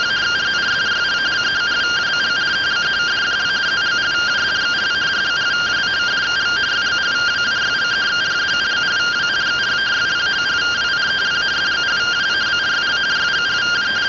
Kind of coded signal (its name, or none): two-tone teleprinter signal